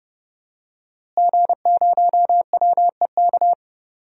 Morse code, heard as a keyed tone, about 30 wpm, 700 Hz. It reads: G0WEK
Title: DXCC-100 Call Signs - 5 Characters - Repeated 2x - 30wpm